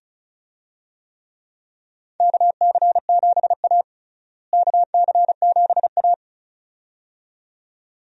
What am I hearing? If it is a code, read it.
Morse code, heard as a keyed tone, about 36 wpm, 700 Hz. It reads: KC7A KC7A
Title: USA Call Signs - 4 Characters - Repeated 2x - 35wpm